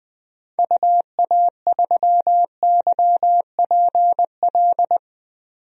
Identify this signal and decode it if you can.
Morse code, heard as a keyed tone, about 21 wpm, 700 Hz. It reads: UA3YPL
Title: DXCC-100 Call Signs - 6 Characters - 20wpm